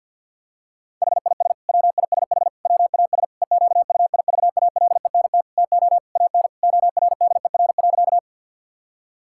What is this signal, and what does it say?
Morse code, heard as a keyed tone, about 50 wpm, 700 Hz. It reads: HIS WISH WAS EQUIVALENT TO AN ORDER.
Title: Sentences from Top 1600 Words 50wpm